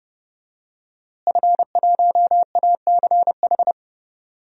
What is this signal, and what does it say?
Morse code, heard as a keyed tone, about 32 wpm, 700 Hz. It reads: F1ACH
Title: DXCC-100 Call Signs - 5 Characters - Repeated 2x - 30wpm